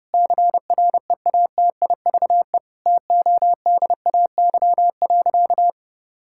Morse code, heard as a keyed tone, about 30 wpm, 700 Hz. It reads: CREATIVE TODAY.
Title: Sentences from Top 1000 Words - Review Part 5 - 30wpm